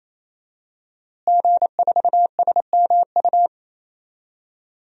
Morse code, heard as a keyed tone, about 27 wpm, 700 Hz. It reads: G4SMU
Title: DXCC-100 Call Signs - 5 Characters - Repeated 3x - 28wpm